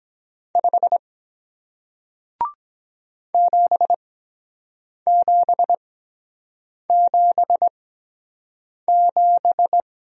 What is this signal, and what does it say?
Morse code, heard as a keyed tone, about 23 wpm, spaced as for 7 wpm, 700 Hz. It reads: 5 7777